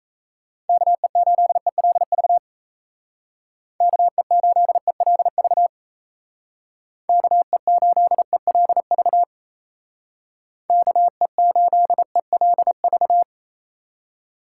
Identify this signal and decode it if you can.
Morse code, heard as a keyed tone, about 34 wpm, 700 Hz. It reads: KE8ELV KE8ELV KE8ELV KE8ELV
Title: Speed Racing - USA 2x3 Call Signs 28wpm